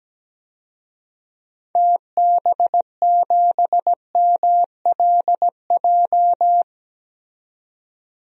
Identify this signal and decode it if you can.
Morse code, heard as a keyed tone, about 17 wpm, 700 Hz. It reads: TB7MLJ
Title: DXCC-100 Call Signs - 6 Characters - 17wpm